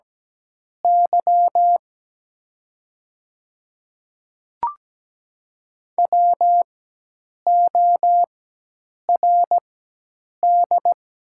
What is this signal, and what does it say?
Morse code, heard as a keyed tone, about 17 wpm, spaced as for 8 wpm, 700 Hz. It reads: Y WORD
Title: 4x Character Spacing - Top 100 Words - 17wpm